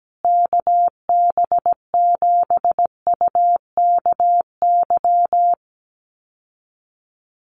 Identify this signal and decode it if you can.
Morse code, heard as a keyed tone, about 17 wpm, 700 Hz. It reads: KB7UKY
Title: USA Call Signs Repeated 2x 17wpm